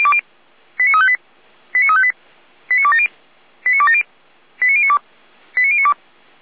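Five-tone selective call (ZVEI), heard as quick runs of five tones, tones of about 70 ms. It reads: 89358 89358 89270 89270 89092 89092
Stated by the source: ZVEI five-tone selective call received by SDR, Feuerwehr Orlamünde.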